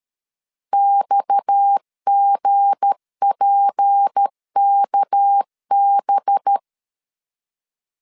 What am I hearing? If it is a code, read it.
Morse code, heard as a keyed tone, about 13 wpm, 800 Hz. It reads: XGPKB
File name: LEZIONI / 7 GF9Z / 7 GF9 12wpm 2319